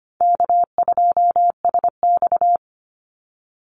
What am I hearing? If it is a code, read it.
Morse code, heard as a keyed tone, about 25 wpm, 700 Hz. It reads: K2SX
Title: Call Signs Repeated 2x - 25wpm